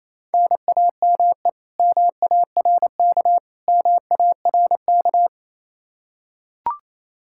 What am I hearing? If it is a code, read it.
Morse code, heard as a keyed tone, about 28 wpm, 700 Hz. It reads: NAME MARK MARK
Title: Practice QSO - Exchange 1 - Beginning - 28wpm